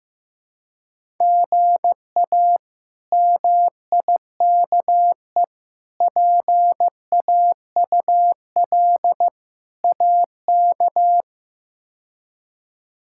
Morse code, heard as a keyed tone, about 16 wpm, 700 Hz. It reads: GA MIKE PAUL AK
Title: K1USN SST QSO Practice - 15wpm